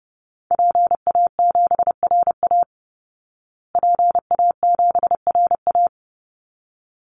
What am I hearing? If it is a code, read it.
Morse code, heard as a keyed tone, about 30 wpm, 700 Hz. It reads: PA7RA PA7RA
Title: Call Signs Repeated 3x - 30wpm